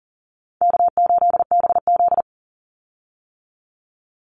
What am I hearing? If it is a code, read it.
Morse code, heard as a keyed tone, about 40 wpm, 700 Hz. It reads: K8BZ
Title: Easy Call Signs - 40wpm